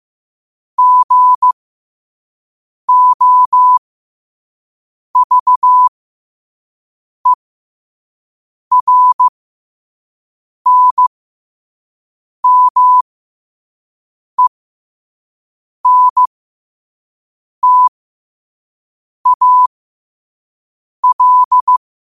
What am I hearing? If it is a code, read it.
Morse code, heard as a keyed tone, about 15 wpm, spaced as for 5 wpm, 1000 Hz. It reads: GOVERNMENTAL
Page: New quotes every day in morse code at 5 Words per minute.